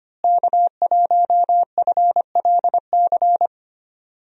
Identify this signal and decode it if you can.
Morse code, heard as a keyed tone, about 25 wpm, 700 Hz. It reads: K1FLC
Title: USA Call Signs - 5 Characters - Repeated 2x - 25wpm